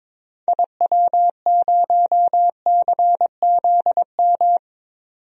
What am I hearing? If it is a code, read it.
Morse code, heard as a keyed tone, about 22 wpm, 700 Hz. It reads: IW0CZM
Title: DXCC-100 Call Signs - 6 Characters - 22wpm